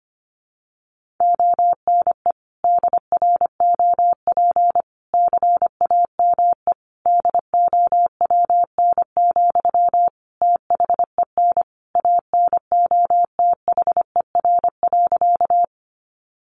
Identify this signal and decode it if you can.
Morse code, heard as a keyed tone, about 25 wpm, 700 Hz. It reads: ONE DROP CAME DOWN, THEN ANOTHER.
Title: Sentences from Top 1400 Words 25wpm